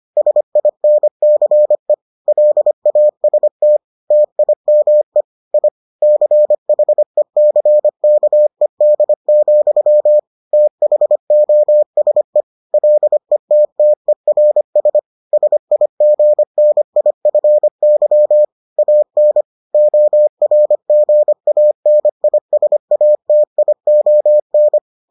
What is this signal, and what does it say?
Morse code, heard as a keyed tone, about 25 wpm, 600 Hz. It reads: SINCE LAST TIME I CHECKED, THOSE LETTERS SIGNIFY AN ORGANISATION